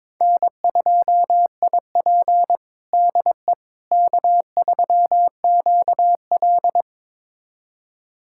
Morse code, heard as a keyed tone, about 22 wpm, 700 Hz. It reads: N2IP DE K3QL
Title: USA Callsign DE Callsign - 4 Characters - 22wpm